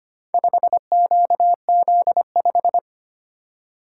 Morse code, heard as a keyed tone, about 25 wpm, 700 Hz. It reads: 5QZ5
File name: practice / 25wpm / Four Letter-Number Combinations - 25wpm